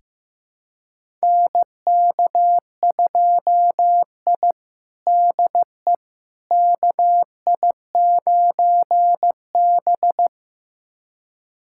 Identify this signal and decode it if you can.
Morse code, heard as a keyed tone, about 15 wpm, 700 Hz. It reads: NK2I DE KI9B
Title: USA Callsign DE Callsign - 4 Characters - 15wpm